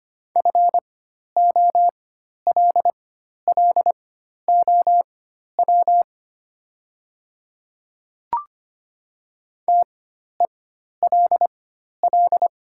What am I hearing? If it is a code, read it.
Morse code, heard as a keyed tone, about 25 wpm, spaced as for 12 wpm, 700 Hz. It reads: FOLLOW TELL
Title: 4x Character Spacing - Top 300 Words - 25wpm